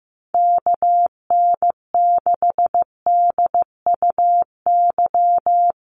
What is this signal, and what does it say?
Morse code, heard as a keyed tone, about 14 wpm, 700 Hz. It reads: KN6DUY
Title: USA Call Signs Repeated 3x 15wpm